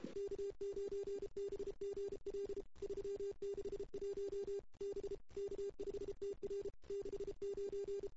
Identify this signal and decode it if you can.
Morse code, heard as a keyed tone, about 31 wpm, 400 Hz. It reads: K9BGL 361 B K5TR 69
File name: CW / 2008-ss-cw-k5tr-op-n5rz -00:00